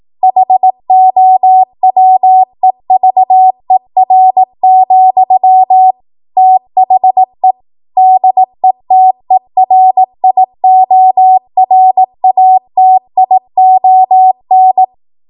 Morse code, heard as a keyed tone, about 18 wpm, 750 Hz. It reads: HOWEVER, THE DETERIORATION
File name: Archive / 18 WPM / 18WPM